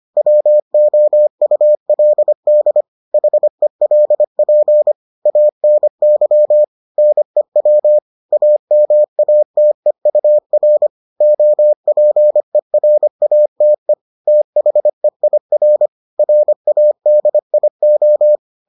Morse code, heard as a keyed tone, about 25 wpm, 600 Hz. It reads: WOULD HELP ANY NEW AMATEUR OPERATE THEIR RADIO